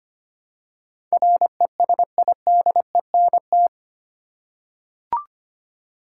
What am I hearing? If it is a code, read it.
Morse code, heard as a keyed tone, about 25 wpm, 690 Hz. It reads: RESIDENT
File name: 8 Letter Words - First 3 Letters Given - 25wpm